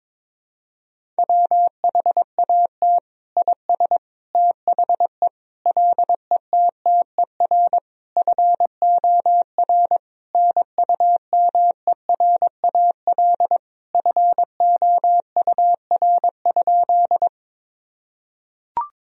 Morse code, heard as a keyed tone, about 22 wpm, 700 Hz. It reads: WHAT IS THE LETTER FOR NUMERAL FOUR?
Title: Sentences from Top 1000 Words - Review Part 2 - 22wpm